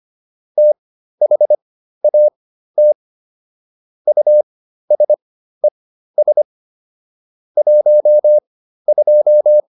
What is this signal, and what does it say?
Morse code, heard as a keyed tone, about 25 wpm, spaced as for 13 wpm, 600 Hz. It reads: THAT USES 12